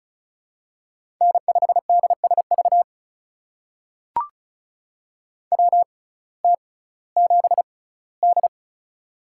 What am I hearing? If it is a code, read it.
Morse code, heard as a keyed tone, about 36 wpm, spaced as for 19 wpm, 700 Hz. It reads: N5DSV WT7D
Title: USA Call Signs - 6x Character Spacing - 35wpm